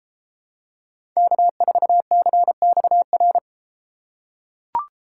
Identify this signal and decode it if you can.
Morse code, heard as a keyed tone, about 32 wpm, 700 Hz. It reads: K4CXR